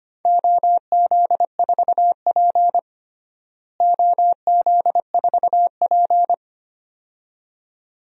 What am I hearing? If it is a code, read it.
Morse code, heard as a keyed tone, about 25 wpm, 700 Hz. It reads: OZ4P OZ4P
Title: DXCC-100 Call Signs - 4 Characters - Repeated 2x - 25wpm